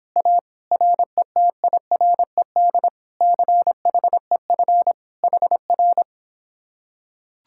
Morse code, heard as a keyed tone, about 26 wpm, 700 Hz. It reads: A RETIRED CHEF HR